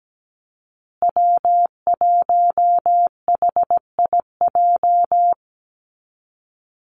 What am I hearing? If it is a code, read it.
Morse code, heard as a keyed tone, about 17 wpm, 700 Hz. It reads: W1HIJ
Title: Call Signs Repeated 3x - 17wpm